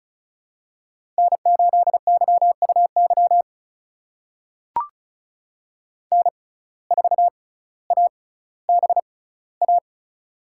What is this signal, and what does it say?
Morse code, heard as a keyed tone, about 34 wpm, spaced as for 18 wpm, 700 Hz. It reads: N8YUY N4ABA
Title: USA Call Signs - 5 Characters - 6x Character Spacing - 35wpm